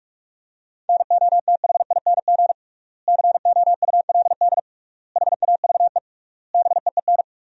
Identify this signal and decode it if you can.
Morse code, heard as a keyed tone, about 48 wpm, 700 Hz. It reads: NOTHING COULD HAVE BEEN
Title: 3x Word-Spacing - Sentences from Top 500 Words - 45wpm